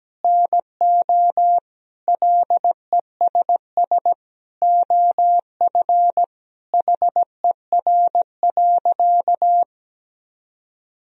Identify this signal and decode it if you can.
Morse code, heard as a keyed tone, about 17 wpm, 700 Hz. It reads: NO LESS OF HER.
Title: Sentences from Top 1000 Words - Review Part 2 - 17wpm